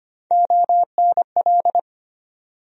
Morse code, heard as a keyed tone, about 25 wpm, 700 Hz. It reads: ONL
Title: USA Call Sign Postfixes - 3 Letters 25wpm